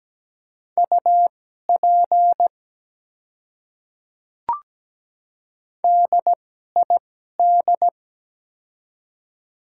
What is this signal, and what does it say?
Morse code, heard as a keyed tone, about 17 wpm, spaced as for 12 wpm, 700 Hz. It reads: UP DID
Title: 2x Character Spacing - Top 100 Words - 17wpm